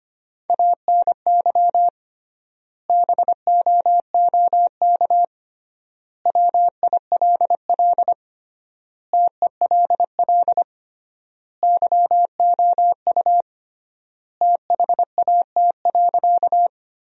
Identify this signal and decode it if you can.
Morse code, heard as a keyed tone, about 25 wpm, 700 Hz. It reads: ANY BOOK WILL TELL YOU THAT.
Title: 3x Word-Spacing - Sentences from Top 300 Words - 25wpm